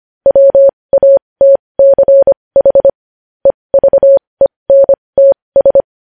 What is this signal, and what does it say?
Morse code, heard as a keyed tone, about 25 wpm, spaced as for 20 wpm, 550 Hz. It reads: WATCH EVENTS
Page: News Headlines in Morse Code at 20 WPM